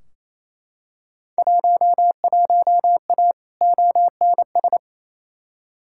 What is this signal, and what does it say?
Morse code, heard as a keyed tone, about 28 wpm, 700 Hz. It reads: 11A ONS
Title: Field Day Exchanges Encore - 28wpm